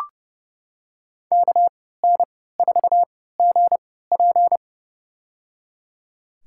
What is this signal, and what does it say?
Morse code, heard as a keyed tone, about 32 wpm, spaced as for 17 wpm, 700 Hz. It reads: KN4GP
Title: USA Call Signs - 3x Character Spacing - 30wpm